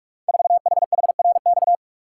Morse code, heard as a keyed tone, about 43 wpm, 700 Hz. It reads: 4HHRX
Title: Code Groups - 5 Letter:Numbers - 45wpm